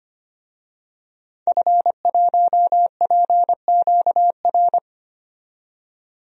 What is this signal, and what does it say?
Morse code, heard as a keyed tone, about 25 wpm, 700 Hz. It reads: F1PQR
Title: DXCC-100 Call Signs - 5 Characters - 25wpm